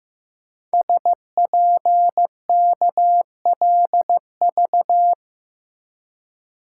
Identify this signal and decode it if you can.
Morse code, heard as a keyed tone, about 15 wpm, 700 Hz. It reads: SPKLV
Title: Code Groups - 5 Letters - 15wpm